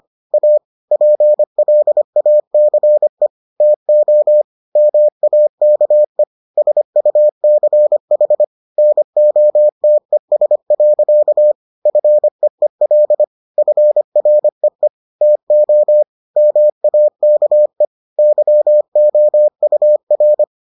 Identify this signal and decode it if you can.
Morse code, heard as a keyed tone, about 25 wpm, 600 Hz. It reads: A PLACE TO MAKE SUCH NOTES. FEEL FREE TO MAKE YOUR